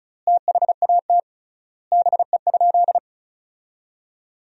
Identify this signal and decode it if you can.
Morse code, heard as a keyed tone, about 36 wpm, 700 Hz. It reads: THAT BE?
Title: 3x Word-Spacing - Sentences from Top 100 Words - 35wpm